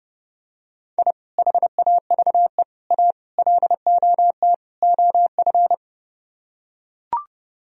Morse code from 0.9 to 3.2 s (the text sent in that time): I HAVE A